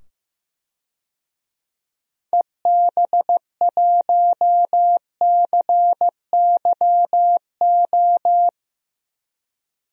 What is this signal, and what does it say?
Morse code, heard as a keyed tone, about 15 wpm, 700 Hz. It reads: EB1CYO
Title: DXCC-100 Call Signs - 6 Characters - 15wpm